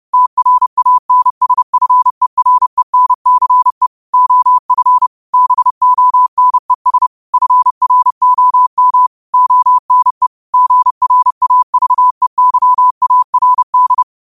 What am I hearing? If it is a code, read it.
Morse code, heard as a keyed tone, about 30 wpm, 1000 Hz. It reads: TRANSFERENCE OF BONES FROM ONE GRAVEYARD
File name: qotdfiles / 30wpm